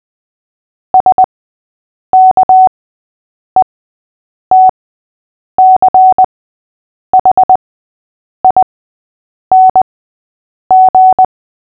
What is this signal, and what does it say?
Morse code, heard as a keyed tone, about 20 wpm, spaced as for 8 wpm, 750 Hz. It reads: SKETCHING
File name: cw / fists / Short-Story-6-8wpm